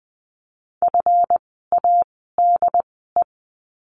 Morse code, heard as a keyed tone, about 20 wpm, spaced as for 15 wpm, 700 Hz. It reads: FADE
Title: 139 Lesson 23f - Letter f - Words 20wpm